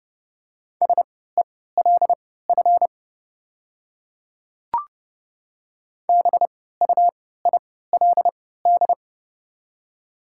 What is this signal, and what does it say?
Morse code, heard as a keyed tone, about 30 wpm, spaced as for 17 wpm, 690 Hz. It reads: SELF BUILD
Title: 3x Character Spacing - Top 200 Words - 30wpm